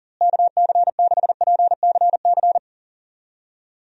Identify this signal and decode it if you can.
Morse code, heard as a keyed tone, about 40 wpm, 700 Hz. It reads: KC6PCC
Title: USA Call Signs - 6 Characters - Repeated 3x - 40wpm